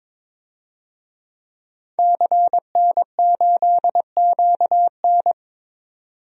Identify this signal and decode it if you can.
Morse code, heard as a keyed tone, about 22 wpm, 700 Hz. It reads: CN8QN
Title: DXCC-100 Call Signs - 5 Characters - 22wpm